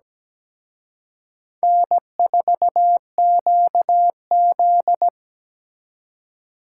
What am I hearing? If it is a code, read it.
Morse code, heard as a keyed tone, about 20 wpm, 700 Hz. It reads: N4QZ
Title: Speed Racing - USA 1x2 Call Signs 17wpm